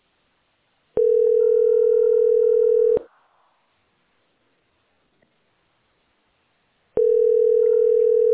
Ringback tone (North America), 440 + 480 Hz.